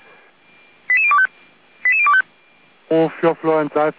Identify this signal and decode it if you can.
Five-tone selective call (ZVEI), heard as quick runs of five tones, tones of about 70 ms, with speech. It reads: 88025 88025